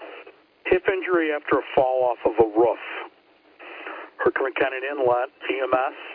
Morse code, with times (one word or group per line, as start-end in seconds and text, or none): none